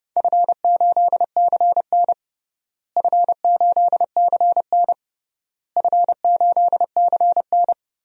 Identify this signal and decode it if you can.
Morse code, heard as a keyed tone, about 32 wpm, 700 Hz. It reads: F8CN F8CN F8CN
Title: DXCC-100 Call Signs - 4 Characters - Repeated 3x - 30wpm